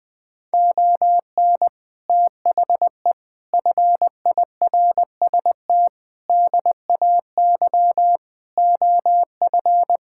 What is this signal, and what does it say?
Morse code, heard as a keyed tone, about 20 wpm, 700 Hz. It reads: ON THE FIRST DAY OF
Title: Sets of 6 Words - 20wpm